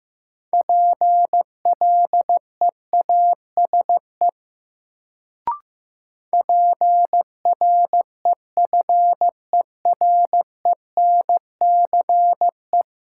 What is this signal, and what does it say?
Morse code, heard as a keyed tone, about 16 wpm, 700 Hz. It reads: PLEASE PREFERENCE
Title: Sentences from Top 1500 Words - New Word Drill Alt - 15wpm